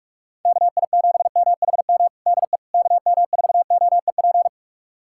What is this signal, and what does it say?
Morse code, heard as a keyed tone, about 41 wpm, 700 Hz. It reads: KI7MHM DE KM4OEP
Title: USA Callsign DE Callsign - 6 Characters - 45wpm